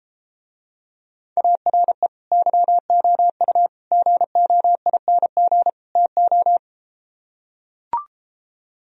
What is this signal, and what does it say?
Morse code, heard as a keyed tone, about 33 wpm, 700 Hz. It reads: ARE YOU GOING TO